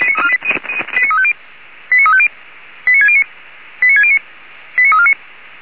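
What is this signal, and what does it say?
Five-tone selective call (ZVEI), heard as quick runs of five tones, tones of about 70 ms. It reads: 89350 89350 89609 89609 89349
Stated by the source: ZVEI five-tone call received by SDR, from the audio server of Feuerwehr Orlamünde (fire brigade).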